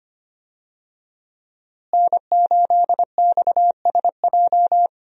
Morse code, heard as a keyed tone, about 25 wpm, 700 Hz. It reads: N8XSJ
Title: USA Call Signs - 5 Characters - Repeated 3x - 25wpm